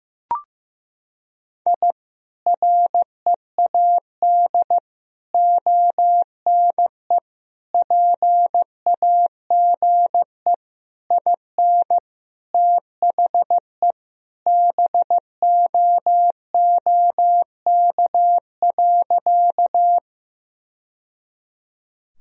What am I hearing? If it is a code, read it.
Morse code, heard as a keyed tone, about 15 wpm, 700 Hz. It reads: I READ ONE PAGE IN THE BOOK.
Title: Sentences from Top 1000 Words - Review Part 1 - 15wpm